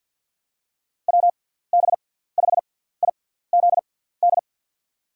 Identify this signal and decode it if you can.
Morse code, heard as a keyed tone, about 48 wpm, spaced as for 17 wpm, 700 Hz. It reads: WB5IZD
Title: USA Call Signs - 6 Characters - 6x Character Spacing - 50wpm